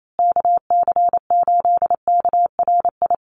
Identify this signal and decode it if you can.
Morse code, heard as a keyed tone, about 28 wpm, 700 Hz. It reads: KC8KRI
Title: USA Call Signs Repeated 3x 28wpm